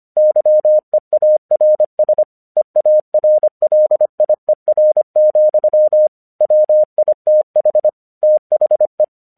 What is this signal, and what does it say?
Morse code, heard as a keyed tone, about 25 wpm, 610 Hz. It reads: YEARS EARLIER, WITH THE
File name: 1 English 25wpm / AQC06